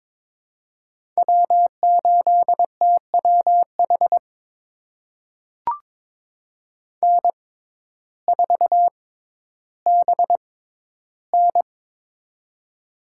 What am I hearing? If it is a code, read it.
Morse code, heard as a keyed tone, about 22 wpm, spaced as for 12 wpm, 700 Hz. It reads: W8TWH N4BN
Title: USA Call Signs - 6x Character Spacing - 22wpm